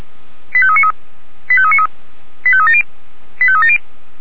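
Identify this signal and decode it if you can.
Five-tone selective call (ZVEI), heard as quick runs of five tones, tones of about 70 ms. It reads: 85393 85393 85370 85370
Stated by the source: ZVEI five-tone selective call received on an SDR.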